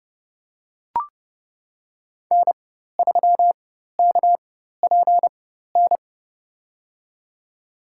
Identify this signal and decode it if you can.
Morse code, heard as a keyed tone, about 29 wpm, spaced as for 14 wpm, 700 Hz. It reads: N3KPN
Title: USA Call Signs - 5 Characters - 4x Character Spacing - 30wpm